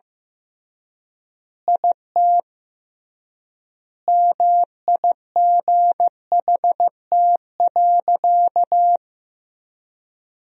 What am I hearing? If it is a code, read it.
Morse code, heard as a keyed tone, about 15 wpm, 700 Hz. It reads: IT MIGHT.
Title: 3x Word-Spacing - Sentences from Top 500 Words - 15wpm